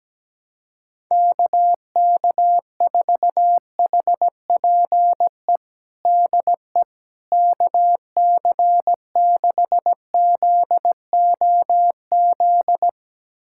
Morse code, heard as a keyed tone, about 17 wpm, 700 Hz. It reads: KK4HPE DE KC6ZOZ
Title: USA Callsign DE Callsign - 6 Characters - 17wpm